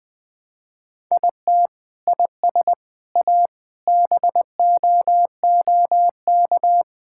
Morse code, heard as a keyed tone, about 21 wpm, 700 Hz. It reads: IT IS A BOOK
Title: Sentences from Top 1000 Words - Review Part 5 - 20wpm